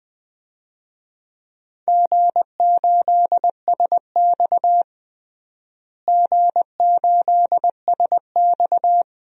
Morse code, heard as a keyed tone, about 20 wpm, 700 Hz. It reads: G8SX G8SX
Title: DXCC-100 Call Signs - 4 Characters - Repeated 2x - 20wpm